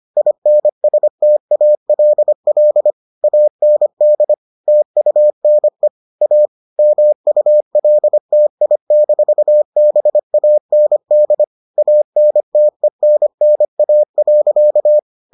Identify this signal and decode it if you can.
Morse code, heard as a keyed tone, about 25 wpm, 600 Hz. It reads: INSTALL AND TUNE A MULTI-BAND ANTENNA.